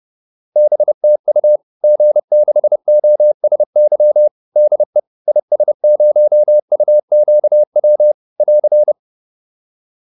Morse code, heard as a keyed tone, about 30 wpm, 600 Hz. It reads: BTU G6OSY DE IS0UQW <AR>